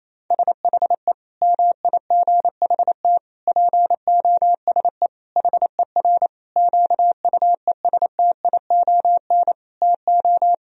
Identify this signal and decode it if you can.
Morse code, heard as a keyed tone, about 28 wpm, 700 Hz. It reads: SHE MIGHT POSE HER QUESTION TO
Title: Sentences from Top 1000 Words - Review Part 2 - 28wpm